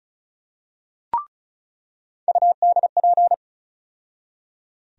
Morse code, heard as a keyed tone, about 36 wpm, 700 Hz. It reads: UDP
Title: USA Call Sign Postfixes - 3 Letters 35wpm